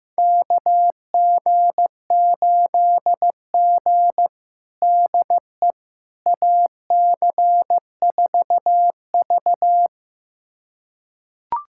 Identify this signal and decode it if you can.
Morse code, heard as a keyed tone, about 15 wpm, 700 Hz. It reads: KG8G DE AC4V
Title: USA Callsign DE Callsign - 4 Characters - 15wpm